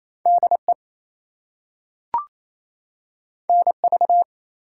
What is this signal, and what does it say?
Morse code, heard as a keyed tone, about 28 wpm, 700 Hz. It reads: DE NV
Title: US States and Canadian Provinces 28wpm